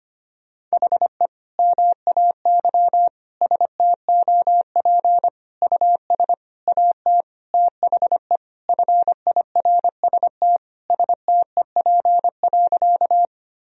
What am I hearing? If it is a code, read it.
Morse code, heard as a keyed tone, about 25 wpm, 700 Hz. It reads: HE MAY STOP US AT THE FIRST STEP.
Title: Sentences from Top 1000 Words - Review Part 2 - 25wpm